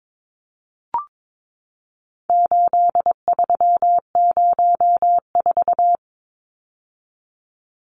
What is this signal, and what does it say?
Morse code, heard as a keyed tone, about 22 wpm, 700 Hz. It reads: 8304
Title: Numbers - 4-Digit 22wpm